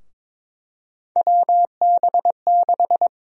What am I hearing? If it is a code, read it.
Morse code, heard as a keyed tone, about 22 wpm, 700 Hz. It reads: WB6
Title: USA Call Sign Prefixes - Repeated 2x - 22wpm